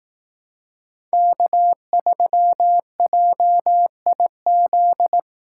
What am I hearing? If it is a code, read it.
Morse code, heard as a keyed tone, about 18 wpm, 700 Hz. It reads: K3JIZ
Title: Speed Racing - USA Call Signs 15wpm